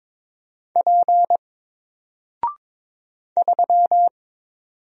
Morse code, heard as a keyed tone, about 22 wpm, 700 Hz. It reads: P 3
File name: Single-Letter-Number-22wpm